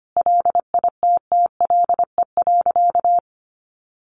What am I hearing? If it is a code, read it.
Morse code, heard as a keyed tone, about 25 wpm, 700 Hz. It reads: LITTLE.
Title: Sentences from Top 1500 Words 25wpm